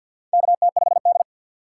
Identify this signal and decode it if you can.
Morse code, heard as a keyed tone, about 48 wpm, 700 Hz. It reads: KT5D
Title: USA Call Signs - 4 Characters - 2x Character Spacing - 50wpm